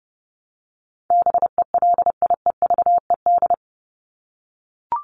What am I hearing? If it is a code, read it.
Morse code, heard as a keyed tone, about 28 wpm, 700 Hz. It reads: BELIEVED
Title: ED Suffix - Top 100 Words with Base Words 30wpm